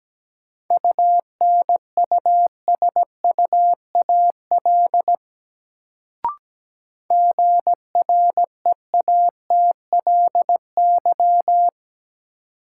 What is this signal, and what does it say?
Morse code, heard as a keyed tone, about 17 wpm, 700 Hz. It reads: UNUSUAL GREATLY
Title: Sentences from Top 1400 Words - New Word Drill Alt - 17wpm